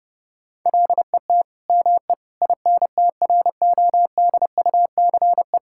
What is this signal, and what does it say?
Morse code, heard as a keyed tone, about 30 wpm, 700 Hz. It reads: LET ME INTRODUCE
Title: Sentences from Top 1600 Words 30wpm